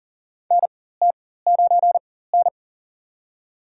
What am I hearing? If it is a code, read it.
Morse code, heard as a keyed tone, about 40 wpm, spaced as for 19 wpm, 700 Hz. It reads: NT9N
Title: USA Call Signs - 4 Characters - 4x Character Spacing - 40wpm